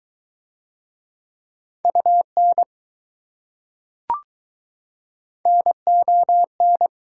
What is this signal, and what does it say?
Morse code, heard as a keyed tone, about 20 wpm, 700 Hz. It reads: UN NON